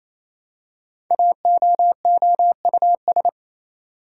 Morse code, heard as a keyed tone, about 27 wpm, 700 Hz. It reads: AOOUS
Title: Code Groups - 5 Letters - 28wpm